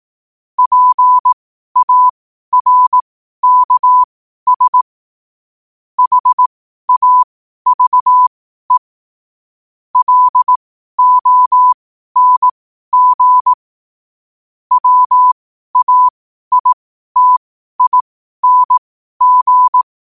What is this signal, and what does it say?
Morse code, heard as a keyed tone, about 18 wpm, spaced as for 12 wpm, 1000 Hz. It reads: PARKS HAVE LONG WAITING
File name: qotdfiles / 12wpm